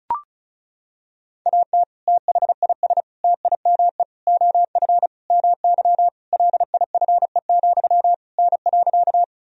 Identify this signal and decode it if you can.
Morse code, heard as a keyed tone, about 34 wpm, 700 Hz. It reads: AT THIS TIME OF MY LIFE, N.
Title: Sentences from Top 300 Words and Abbreviations 35wpm